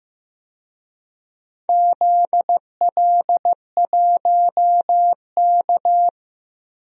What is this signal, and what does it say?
Morse code, heard as a keyed tone, about 15 wpm, 700 Hz. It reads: ZL1K
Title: DXCC-100 Call Signs - 4 Characters - 15wpm